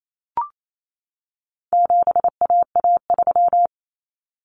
Morse code, heard as a keyed tone, about 28 wpm, 700 Hz. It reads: 7AA3
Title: Common Cut Numbers - 4-Digit 28wpm